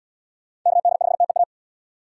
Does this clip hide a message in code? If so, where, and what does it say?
Yes: Morse code from 0.7 to 1.4 s, heard as a keyed tone, about 71 wpm, 700 Hz. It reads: KD5TEA